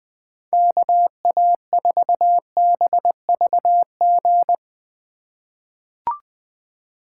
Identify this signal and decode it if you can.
Morse code, heard as a keyed tone, about 20 wpm, 700 Hz. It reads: KA4BVG
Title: K1USN SST QSO Practice - 20wpm